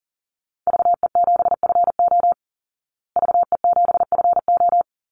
Morse code, heard as a keyed tone, about 40 wpm, 700 Hz. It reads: VE7FO VE7FO
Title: Call Signs Repeated 3x - 40wpm